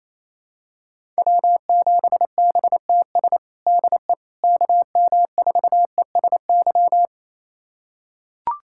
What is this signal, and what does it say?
Morse code, heard as a keyed tone, about 28 wpm, 700 Hz. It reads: W7BTS DE KM4ESY
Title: USA Callsign DE Callsign 28wpm